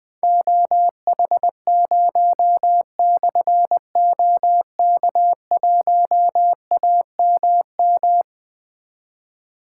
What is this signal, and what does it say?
Morse code, heard as a keyed tone, about 20 wpm, 700 Hz. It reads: OH0/OK1AMM
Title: Difficult Call Signs 20wpm